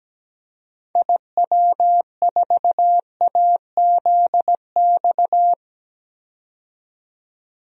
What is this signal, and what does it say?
Morse code, heard as a keyed tone, about 17 wpm, 700 Hz. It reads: IW4AZX
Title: DXCC-100 Call Signs - 6 Characters - 17wpm